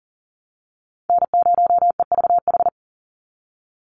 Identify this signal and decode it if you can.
Morse code, heard as a keyed tone, about 40 wpm, 700 Hz. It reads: N0EVH
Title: NA SOTA Chaser Callsigns 40wpm